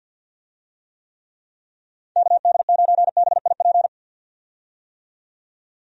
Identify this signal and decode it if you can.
Morse code, heard as a keyed tone, about 52 wpm, 700 Hz. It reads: KD9BIP
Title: USA Call Signs - 6 Characters - 50wpm